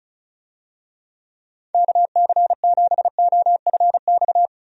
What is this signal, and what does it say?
Morse code, heard as a keyed tone, about 36 wpm, 700 Hz. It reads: KC7OFX
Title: USA Call Signs - 6 Characters - Repeated 3x - 35wpm